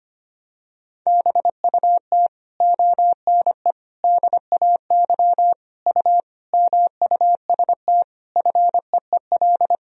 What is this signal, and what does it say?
Morse code, heard as a keyed tone, about 25 wpm, 700 Hz. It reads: BUT ONE DAY U MUST FEEL
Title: Sentences from Top 400 Words and Abbreviations 25wpm